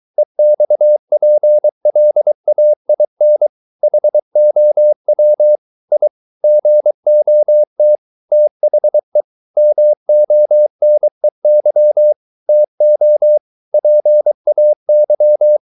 Morse code, heard as a keyed tone, about 23 wpm, 600 Hz. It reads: EXPLAIN HOW I GOT THE MONEY TO PAY